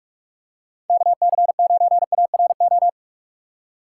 Morse code, heard as a keyed tone, about 46 wpm, 700 Hz. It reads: KC9ARO
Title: USA Call Signs - 6 Characters - 6x Character Spacing - 45wpm